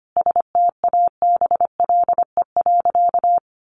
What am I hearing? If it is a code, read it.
Morse code, heard as a keyed tone, about 25 wpm, 700 Hz. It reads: STABLE.
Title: Sentences from Top 1500 Words 25wpm